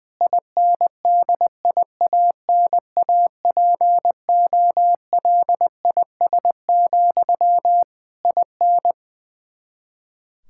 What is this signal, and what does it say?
Morse code, heard as a keyed tone, about 20 wpm, 700 Hz. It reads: INDIANAPOLIS, IN
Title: K1USN SST QSO Practice 20wpm